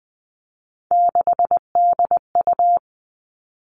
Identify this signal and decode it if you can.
Morse code, heard as a keyed tone, about 20 wpm, 700 Hz. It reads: 6DU
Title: All Cut Numbers - 3-Digit 20wpm